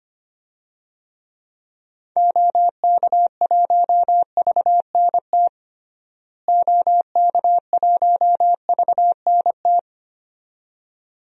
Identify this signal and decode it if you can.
Morse code, heard as a keyed tone, about 25 wpm, 700 Hz. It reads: OK1VNT OK1VNT
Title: DXCC-100 Call Signs - 6 Characters - Repeated 2x - 25wpm